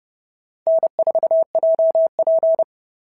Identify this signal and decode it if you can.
Morse code, heard as a keyed tone, about 30 wpm, 650 Hz. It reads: N4JP
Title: SOTA QSO Practice 30wpm